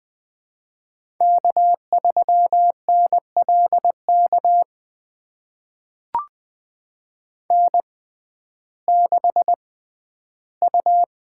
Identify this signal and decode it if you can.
Morse code, heard as a keyed tone, about 20 wpm, spaced as for 8 wpm, 700 Hz. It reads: K3NL KN6U
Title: USA Call Signs - 6x Character Spacing - 20wpm